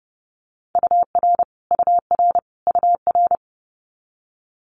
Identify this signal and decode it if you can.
Morse code, heard as a keyed tone, about 32 wpm, 700 Hz. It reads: UR UR UR
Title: Multi-letter Phonemes Repeated 3x 30wpm